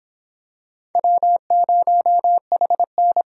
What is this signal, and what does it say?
Morse code, heard as a keyed tone, about 26 wpm, 700 Hz. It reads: W0HN